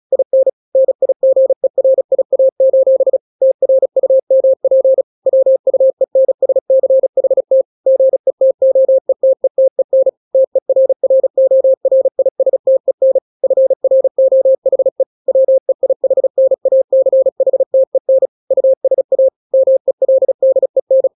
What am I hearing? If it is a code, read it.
Morse code, heard as a keyed tone, about 36 wpm, 530 Hz. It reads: IN NIGERIA: TRUMP WUENSCHT GETOETETEN TERRORISTEN FROHE WEIHNACHTEN USA MELDEN
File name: daily / 35wpm